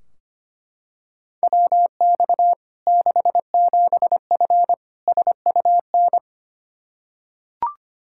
Morse code, heard as a keyed tone, about 25 wpm, 700 Hz. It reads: WX 67F SUN
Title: K1USN SST QSO Practice 25wpm